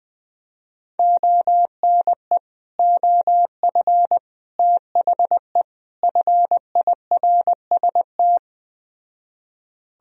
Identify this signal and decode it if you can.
Morse code, heard as a keyed tone, about 20 wpm, 700 Hz. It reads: ONE OF THE FIRST